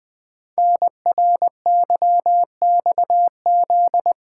Morse code, heard as a keyed tone, about 20 wpm, 700 Hz. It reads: NRYXZ
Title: Code Groups - 5 Letters - 20wpm